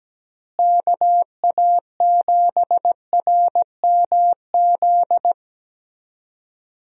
Morse code, heard as a keyed tone, about 17 wpm, 700 Hz. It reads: KA7RMZ
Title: USA Call Signs - 6 Characters - Repeated 2x - 17wpm